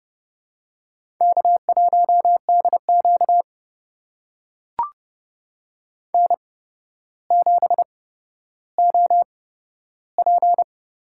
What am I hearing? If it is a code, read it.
Morse code, heard as a keyed tone, about 30 wpm, spaced as for 13 wpm, 700 Hz. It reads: K1DQ N7OP
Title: USA Call Signs - 4 Characters - 8x Character Spacing - 30wpm